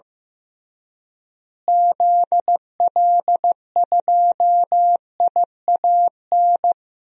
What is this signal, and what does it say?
Morse code, heard as a keyed tone, about 15 wpm, 700 Hz. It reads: ZL2IAN
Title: DXCC-100 Call Signs - 6 Characters - Repeated 3x - 15wpm